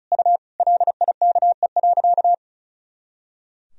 Morse code, heard as a keyed tone, about 34 wpm, 700 Hz. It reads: U LIKE.
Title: Sentences from Top 500 Words and Abbreviations 35wpm